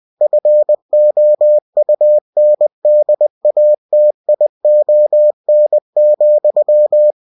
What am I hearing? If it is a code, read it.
Morse code, heard as a keyed tone, about 20 wpm, 600 Hz. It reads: FOUNDATION,
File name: cw-books / winnie-the-pooh / winnie-20wpm-page-0120